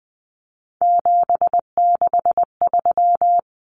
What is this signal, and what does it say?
Morse code, heard as a keyed tone, about 19 wpm, 700 Hz. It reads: 763
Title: Numbers - 3-Digit 20wpm